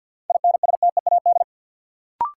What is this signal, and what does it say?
Morse code, heard as a keyed tone, about 53 wpm, 700 Hz. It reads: INSTEAD
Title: Sentences from Top 800 Words - New Word Drill - 50wpm